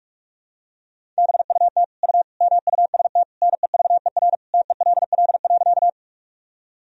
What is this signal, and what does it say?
Morse code, heard as a keyed tone, about 46 wpm, 700 Hz. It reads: BUT U MUST NEVER TELL.
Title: Sentences from Top 300 Words and Abbreviations 45wpm